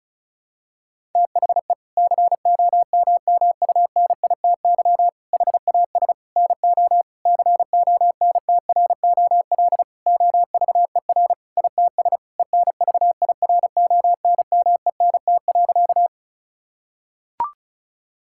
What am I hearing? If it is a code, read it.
Morse code, heard as a keyed tone, about 36 wpm, 700 Hz. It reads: THE COMMUNITY HAS NO CONTROL OVER ITS ENVIRONMENT.
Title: Sentences from Top 1000 Words - Review Part 5 - 35wpm